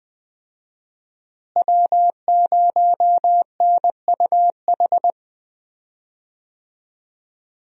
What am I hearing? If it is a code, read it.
Morse code, heard as a keyed tone, about 21 wpm, 700 Hz. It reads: W0NUH
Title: USA Call Signs - 5 Characters - Repeated 2x - 20wpm